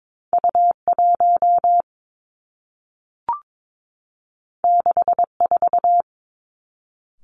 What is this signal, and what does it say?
Morse code, heard as a keyed tone, about 22 wpm, 700 Hz. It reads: U 1 6 4
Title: Two Challenging Letter-Number Combinations 22wpm